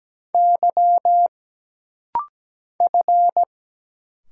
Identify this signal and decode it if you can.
Morse code, heard as a keyed tone, about 17 wpm, 700 Hz. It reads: Y F
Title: Single Letters - Rapid-Fire 17wpm